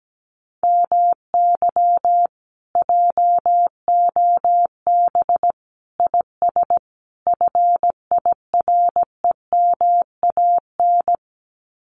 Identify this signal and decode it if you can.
Morse code, heard as a keyed tone, about 17 wpm, 700 Hz. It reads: MY JOB IS FIREMAN
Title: Common QSO Phrases Encore 17wpm